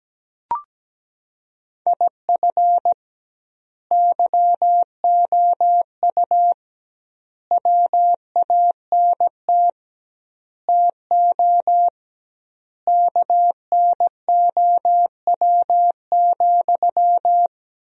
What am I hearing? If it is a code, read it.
Morse code, heard as a keyed tone, about 17 wpm, 700 Hz. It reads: IF YOU WANT TO KNOW,
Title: 2x Word-Spacing - Sentences from Top 200 Words - 17wpm